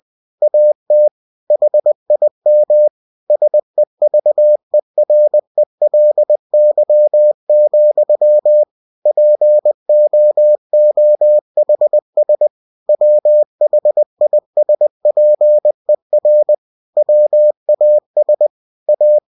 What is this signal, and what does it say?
Morse code, heard as a keyed tone, about 20 wpm, 600 Hz. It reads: AT HIM SEVERELY, POOHS WHISPER WAS A
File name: cw-books / winnie-the-pooh / winnie-20wpm-page-0082